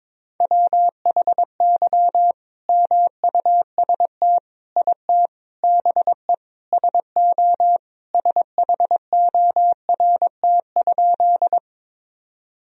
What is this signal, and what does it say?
Morse code, heard as a keyed tone, about 22 wpm, 700 Hz. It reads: WHY MUST IT BE SO SHORT?
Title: Sentences from Top 1000 Words - Review Part 2 - 22wpm